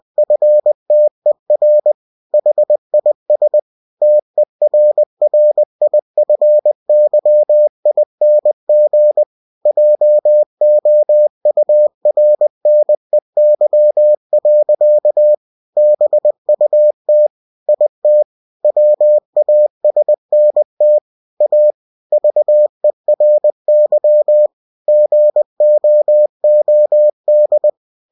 Morse code, heard as a keyed tone, about 20 wpm, 600 Hz. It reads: FTER HIS TERRIFYING JOURNEY. BUT IT WASNT A VERY GOOD
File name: cw-books / winnie-the-pooh / winnie-20wpm-page-0071